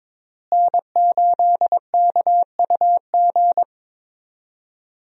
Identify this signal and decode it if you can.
Morse code, heard as a keyed tone, about 22 wpm, 700 Hz. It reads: N8KUG
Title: USA Call Signs - 5 Characters - Repeated 3x - 22wpm